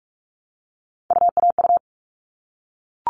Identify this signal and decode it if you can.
Morse code, heard as a keyed tone, about 44 wpm, 700 Hz. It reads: UAU
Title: All Cut Numbers - 3-Digit 45wpm